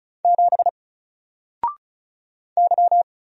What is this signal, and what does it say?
Morse code, heard as a keyed tone, about 36 wpm, 700 Hz. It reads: 7 Y
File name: Single-Letter-Number-35wpm